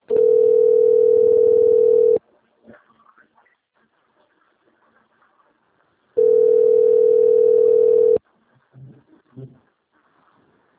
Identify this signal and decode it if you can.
Ringback tone (North America), 440 + 480 Hz.
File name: rg-1600-unknown- - - .31120